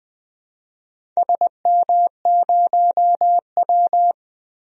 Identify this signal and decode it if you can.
Morse code, heard as a keyed tone, about 20 wpm, 700 Hz. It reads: SM0W
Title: DXCC-100 Call Signs - 4 Characters - Repeated 3x - 20wpm